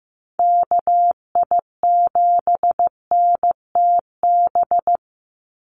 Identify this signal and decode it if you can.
Morse code, heard as a keyed tone, about 15 wpm, 700 Hz. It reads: KI7NTB
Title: USA Call Signs 15wpm